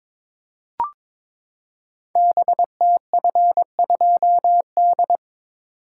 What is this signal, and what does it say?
Morse code, heard as a keyed tone, about 22 wpm, 700 Hz. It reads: BTF2D
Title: Code Groups - 5 Letter:Numbers - 22wpm